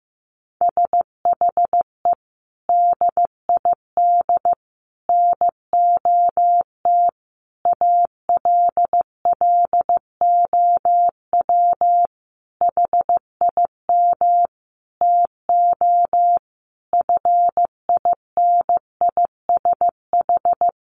Morse code, heard as a keyed tone, about 15 wpm, 700 Hz. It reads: SHE DID NOT ALLOW HIM TO FINISH
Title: Sentences from Top 1200 Words 15wpm